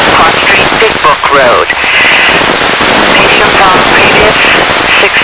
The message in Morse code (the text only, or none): none